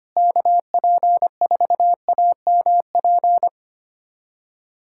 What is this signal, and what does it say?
Morse code, heard as a keyed tone, about 25 wpm, 700 Hz. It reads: KP4AMP
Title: USA Callsign DE Callsign 25wpm